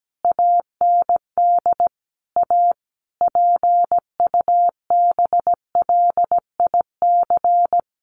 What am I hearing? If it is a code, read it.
Morse code, heard as a keyed tone, about 17 wpm, 700 Hz. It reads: AND A PUBLIC
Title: Sentences from Top 1300 Words 17wpm